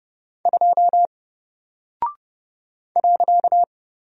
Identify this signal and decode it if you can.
Morse code, heard as a keyed tone, about 30 wpm, 700 Hz. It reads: 2 .
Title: Single Character (Letter:Number:Punctuation:Prosign) 30wpm